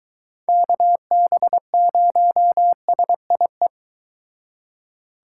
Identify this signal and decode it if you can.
Morse code, heard as a keyed tone, about 23 wpm, 700 Hz. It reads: KB0SIE